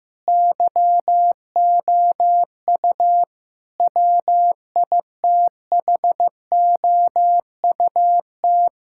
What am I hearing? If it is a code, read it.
Morse code, heard as a keyed tone, about 16 wpm, 700 Hz. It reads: YOU WITHOUT
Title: Sentences from Top 1600 Words 15wpm